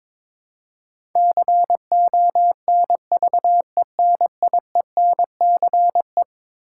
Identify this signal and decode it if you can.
Morse code, heard as a keyed tone, about 22 wpm, 700 Hz. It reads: CONVENIENCE
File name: practice / 22wpm / Eleven Letter Words - 22wpm